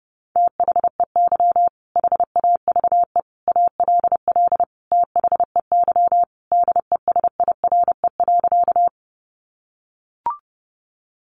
Morse code, heard as a keyed tone, about 30 wpm, 700 Hz. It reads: THEY HAVE ALL THEY DESIRE.
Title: Sentences from Top 1100 Words 30wpm